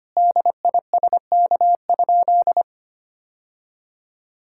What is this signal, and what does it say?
Morse code, heard as a keyed tone, about 25 wpm, 700 Hz. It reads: DISK?
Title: Sentences from Top 1000 Words - Review Part 6 - 25wpm